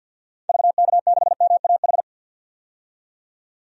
Morse code, heard as a keyed tone, about 48 wpm, 700 Hz. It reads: VK6MAH
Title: DXCC-100 Call Signs - 6 Characters - 50wpm